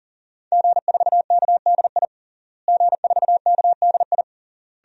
Morse code, heard as a keyed tone, about 40 wpm, 700 Hz. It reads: G4KDI G4KDI
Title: DXCC-100 Call Signs - 5 Characters - Repeated 3x - 40wpm